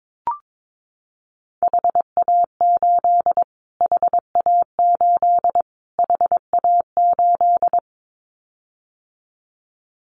Morse code, heard as a keyed tone, about 22 wpm, 700 Hz. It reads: HA8 HA8 HA8
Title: Callsign Prefixes for Top 25 DXCC - Repeated 3x - 22wpm